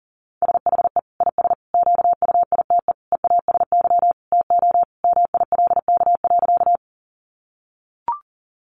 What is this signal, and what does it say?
Morse code, heard as a keyed tone, about 40 wpm, 700 Hz. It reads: SHE IS QUITE EASY TO MILK.
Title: Sentences from Top 1500 Words 40wpm